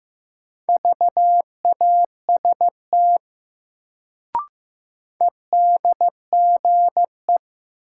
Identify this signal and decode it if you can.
Morse code, heard as a keyed tone, about 15 wpm, 700 Hz. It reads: VAST EDGE
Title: Sentences from Top 1300 Words - New Word Drill - 15wpm